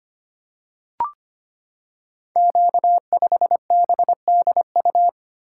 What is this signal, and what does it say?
Morse code, heard as a keyed tone, about 25 wpm, 700 Hz. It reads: Q5BDU
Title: Code Groups - 5 Letter:Numbers - 25wpm